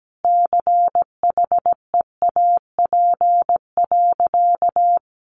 Morse code, heard as a keyed tone, about 17 wpm, 700 Hz. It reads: CHEAP.
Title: Sentences from Top 1200 Words 17wpm